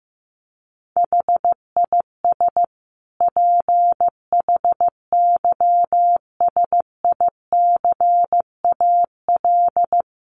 Morse code, heard as a keyed tone, about 15 wpm, 700 Hz. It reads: HIS PHYSICAL
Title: Sentences from Top 1400 Words 15wpm